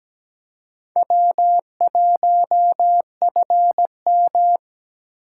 Morse code, heard as a keyed tone, about 17 wpm, 700 Hz. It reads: W1FM
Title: USA Call Signs - 4 Characters - 4x Character Spacing - 17wpm